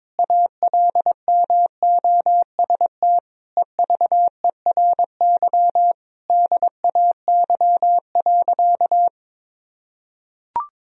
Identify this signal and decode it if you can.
Morse code, heard as a keyed tone, about 21 wpm, 700 Hz. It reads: ALMOST EVERY DAY.
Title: Sentences from Top 1000 Words - Review Part 4 - 22wpm